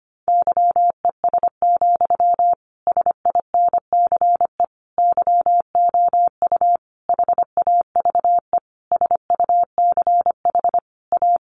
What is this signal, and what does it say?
Morse code, heard as a keyed tone, about 25 wpm, 700 Hz. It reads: YES, SINCE YOU HAVE SUCH A
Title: Sentences from Top 1300 Words 25wpm